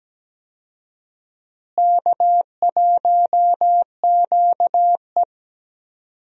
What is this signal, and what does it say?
Morse code, heard as a keyed tone, about 17 wpm, 700 Hz. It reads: K1QE
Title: USA Call Signs - 4 Characters - Repeated 2x - 17wpm